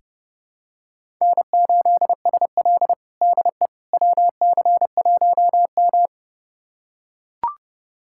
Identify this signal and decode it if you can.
Morse code, heard as a keyed tone, about 30 wpm, 700 Hz. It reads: N8SL DE WC1M
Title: USA Callsign DE Callsign - 4 Characters - 30wpm